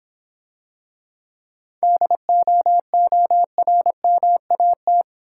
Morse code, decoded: DOORMAT

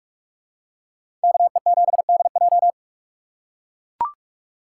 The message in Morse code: KE7DJ

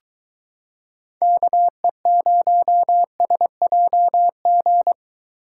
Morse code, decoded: KE0SJG

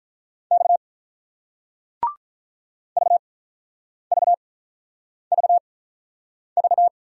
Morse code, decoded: X VVVV

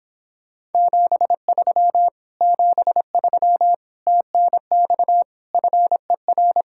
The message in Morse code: 73 73 TNX FER

